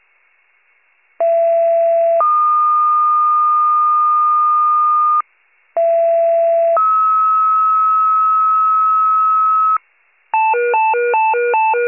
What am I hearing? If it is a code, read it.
Two-tone page, heard as a one-second tone, then a three-second tone, about 670 then 1220 Hz, about 670 then 1290 Hz. It is heard from 1.2 to 9.8 s.